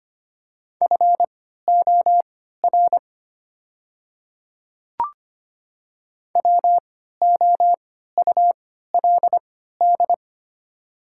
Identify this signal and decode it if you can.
Morse code, heard as a keyed tone, about 25 wpm, spaced as for 14 wpm, 690 Hz. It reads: FOR WOULD
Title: 3x Character Spacing - Top 100 Words - 25wpm